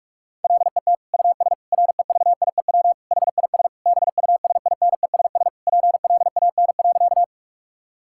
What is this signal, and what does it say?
Morse code, heard as a keyed tone, about 47 wpm, 700 Hz. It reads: LET US REVIEW HIS BUSINESS PLAN.
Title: Sentences from Top 1000 Words - Review Part 6 - 45wpm